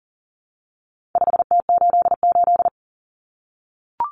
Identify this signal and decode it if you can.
Morse code, heard as a keyed tone, about 40 wpm, 700 Hz. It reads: 5T88
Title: Common Cut Numbers - 4-Digit 40wpm